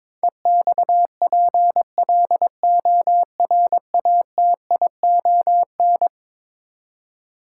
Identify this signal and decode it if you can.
Morse code, heard as a keyed tone, about 21 wpm, 700 Hz. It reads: EXPLORATION
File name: practice / 22wpm / Eleven Letter Words - 22wpm